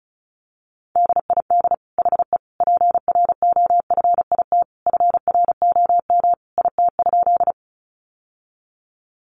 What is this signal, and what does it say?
Morse code, heard as a keyed tone, about 34 wpm, 700 Hz. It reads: DID HE PROFIT FROM IT?
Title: Sentences from Top 800 Words 35wpm